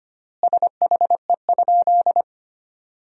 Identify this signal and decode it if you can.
Morse code, heard as a keyed tone, about 25 wpm, 700 Hz. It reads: SHE?
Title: 3x Word-Spacing - Sentences from Top 100 Words - 25wpm